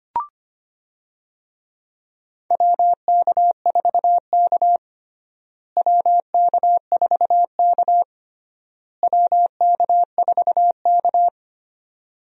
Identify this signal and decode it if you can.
Morse code, heard as a keyed tone, about 25 wpm, 700 Hz. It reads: WK4K WK4K WK4K
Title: USA Call Signs - 4 Characters - Repeated 3x - 25wpm